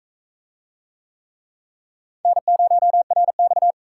Morse code, heard as a keyed tone, about 35 wpm, 700 Hz. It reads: N0RX
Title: Speed Racing - USA 1x2 Call Signs 28wpm